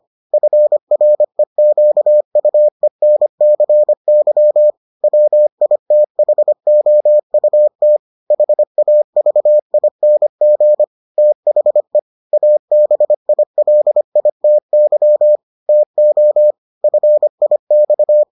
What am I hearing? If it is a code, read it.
Morse code, heard as a keyed tone, about 25 wpm, 600 Hz. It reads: FREQUENCY WITHOUT HAVING THE ABILITY TO FIX